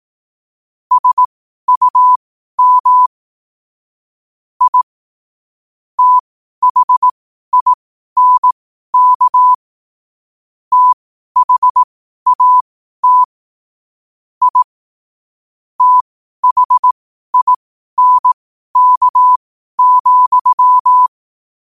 Morse code, heard as a keyed tone, about 18 wpm, spaced as for 12 wpm, 1000 Hz. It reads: SUM I THINK THAT I THINK,
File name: qotdfiles / 12wpm